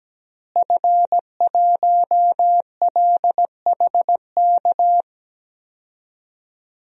Morse code, heard as a keyed tone, about 17 wpm, 700 Hz. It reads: F1LHK
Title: DXCC-100 Call Signs - 5 Characters - Repeated 2x - 17wpm